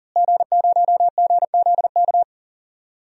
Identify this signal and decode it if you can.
Morse code, heard as a keyed tone, about 40 wpm, 700 Hz. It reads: G0GZK
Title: DXCC-100 Call Signs - 5 Characters - Repeated 3x - 40wpm